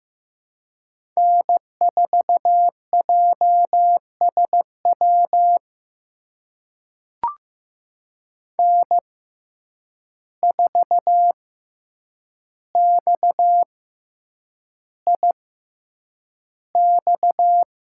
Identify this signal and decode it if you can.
Morse code, heard as a keyed tone, about 15 wpm, spaced as for 8 wpm, 700 Hz. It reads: N4JSW N4XIX
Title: USA Call Signs - 5 Characters - 6x Character Spacing - 15wpm